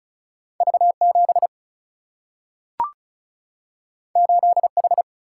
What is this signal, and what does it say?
Morse code, heard as a keyed tone, about 36 wpm, 700 Hz. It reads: V 7 8 H
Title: Single-Letter Single-Number 35wpm